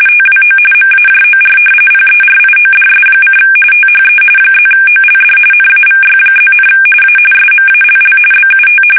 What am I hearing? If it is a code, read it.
RTTY, about 45.45 baud. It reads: TWO MAJOR PERIODS: THE +OLD NAVY+, A SMALL BUT
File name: fsk850 / usnavy1